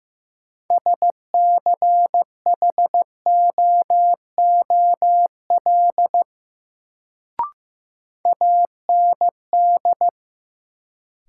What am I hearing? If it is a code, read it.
Morse code, heard as a keyed tone, about 15 wpm, 700 Hz. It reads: SCHOOL AND
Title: Sets of 2 Words - New Word Drill - 15wpm